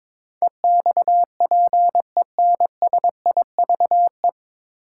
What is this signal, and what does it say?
Morse code, heard as a keyed tone, about 22 wpm, 690 Hz. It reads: EXPENSIVE
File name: 9 Letter Words - First 3 Letters Given - 22wpm